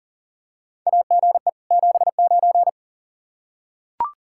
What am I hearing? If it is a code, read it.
Morse code, heard as a keyed tone, about 40 wpm, 700 Hz. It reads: AGE 79
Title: Practice QSO - Exchange 3 - More Info - 40wpm